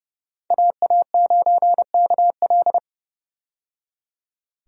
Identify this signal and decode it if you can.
Morse code, heard as a keyed tone, about 30 wpm, 700 Hz. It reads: AA9KL
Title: USA Call Signs - 5 Characters - 30wpm